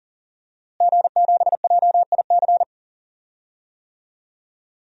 Morse code, heard as a keyed tone, about 39 wpm, 700 Hz. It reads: G7JIC
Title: DXCC-100 Call Signs - 5 Characters - Repeated 2x - 40wpm